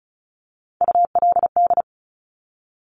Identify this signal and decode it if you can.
Morse code, heard as a keyed tone, about 36 wpm, 690 Hz. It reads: ULD
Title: Top Three Letter Combinations in Words 35wpm